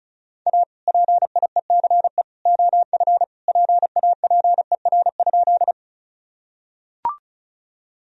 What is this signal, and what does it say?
Morse code, heard as a keyed tone, about 36 wpm, 700 Hz. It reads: A PIECE OF PAPER?
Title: Sentences from Top 1000 Words - Review Part 2 - 35wpm